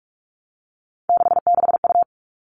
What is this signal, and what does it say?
Morse code, heard as a keyed tone, about 44 wpm, 700 Hz. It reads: 66U
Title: All Cut Numbers - 3-Digit 45wpm